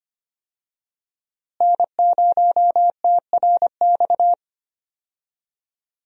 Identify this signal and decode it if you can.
Morse code, heard as a keyed tone, about 25 wpm, 700 Hz. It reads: N0TRX
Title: USA Call Signs - 5 Characters - Repeated 2x - 25wpm